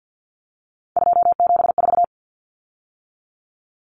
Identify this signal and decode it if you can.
Morse code, heard as a keyed tone, about 49 wpm, 700 Hz. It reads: 274
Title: Numbers - 3-Digit 50wpm